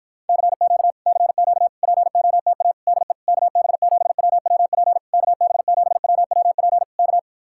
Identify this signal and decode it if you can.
Morse code, heard as a keyed tone, about 52 wpm, 700 Hz. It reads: CQ CQ POTA DE KB7WWP KB7WWP K